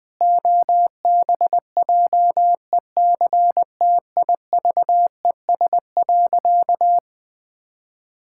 Morse code, heard as a keyed tone, about 20 wpm, 700 Hz. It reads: OBJECTIVES.
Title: Pangram Challenge v2 - 20wpm